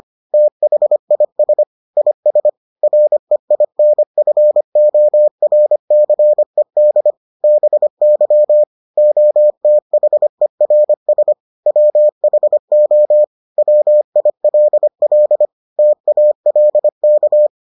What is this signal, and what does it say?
Morse code, heard as a keyed tone, about 25 wpm, 600 Hz. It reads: THIS IS REINFORCED BY OTHERS WHO WILL TALK